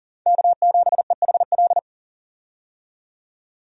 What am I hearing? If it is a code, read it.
Morse code, heard as a keyed tone, about 40 wpm, 700 Hz. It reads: K7EHL
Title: USA Call Signs - 5 Characters - 40wpm